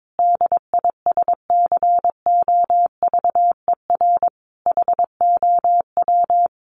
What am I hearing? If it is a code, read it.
Morse code, heard as a keyed tone, about 22 wpm, 700 Hz. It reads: DISCOVER HOW
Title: Sentences from Top 1100 Words 22wpm